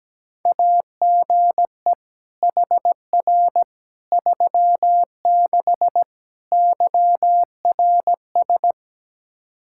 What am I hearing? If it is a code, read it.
Morse code, heard as a keyed tone, about 17 wpm, 700 Hz. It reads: AGE HR 36 YRS
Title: Practice QSO - Exchange 3 - More Info - 17wpm